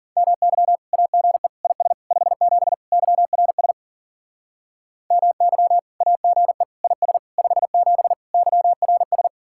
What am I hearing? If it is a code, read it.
Morse code, heard as a keyed tone, about 52 wpm, 700 Hz. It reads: MY AGE IS 57 YRS MY AGE IS 57 YRS